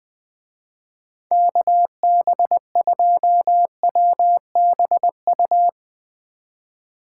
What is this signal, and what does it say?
Morse code, heard as a keyed tone, about 20 wpm, 700 Hz. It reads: KB2WBU
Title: Speed Racing - USA Call Signs 17wpm